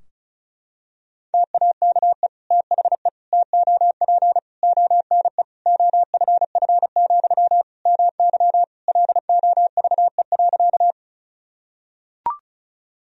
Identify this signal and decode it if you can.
Morse code, heard as a keyed tone, about 36 wpm, 700 Hz. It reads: TAKE THE TOP ONE OFF, MY LOVE.
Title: Sentences from Top 1000 Words - Review Part 2 - 35wpm